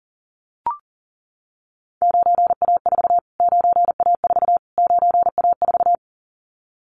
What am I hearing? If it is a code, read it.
Morse code, heard as a keyed tone, about 40 wpm, 700 Hz. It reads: 9A4 9A4 9A4
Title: Callsign Prefixes for Top 25 DXCC - Repeated 3x - 40wpm